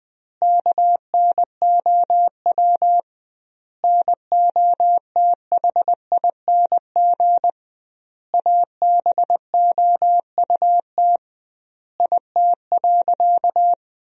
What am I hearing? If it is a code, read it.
Morse code, heard as a keyed tone, about 21 wpm, 700 Hz. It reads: KNOW NOTHING ABOUT IT.
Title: 2x Word-Spacing - Sentences from Top 500 Words - 20wpm